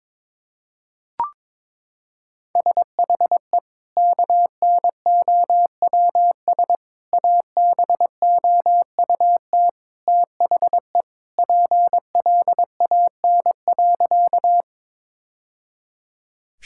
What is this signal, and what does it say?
Morse code, heard as a keyed tone, about 22 wpm, 700 Hz. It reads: SHE KNOWS ABOUT THE PLAN.
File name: practice / 22wpm / Sentences with 2-Word Phrases - 22wpm